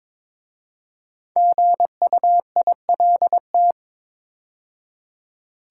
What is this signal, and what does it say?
Morse code, heard as a keyed tone, about 22 wpm, 700 Hz. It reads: GUILT
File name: 5 Letter Words - First 3 Letters Given - 22wpm